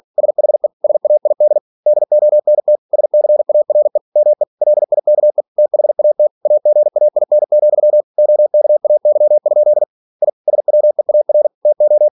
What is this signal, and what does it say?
Morse code, heard as a keyed tone, about 46 wpm, 600 Hz. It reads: SHE SAID DONT SCARE ME LIKE THAT AGAIN, OKAY? I SWEAR TO
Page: Never 47wpm